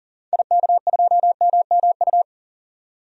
Morse code, heard as a keyed tone, about 39 wpm, 700 Hz. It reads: IK2MMU
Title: DXCC-100 Call Signs - 6 Characters - 40wpm